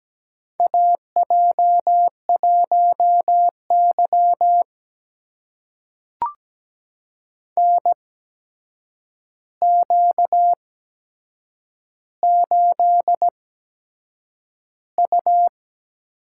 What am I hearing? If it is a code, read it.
Morse code, heard as a keyed tone, about 17 wpm, spaced as for 7 wpm, 700 Hz. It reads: AJ1Y NQ8U
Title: USA Call Signs - 4 Characters - 8x Character Spacing - 17wpm